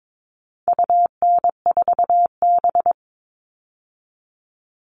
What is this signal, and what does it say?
Morse code, heard as a keyed tone, about 22 wpm, 700 Hz. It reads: UN4B
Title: All Cut Numbers - 4-Digit 22wpm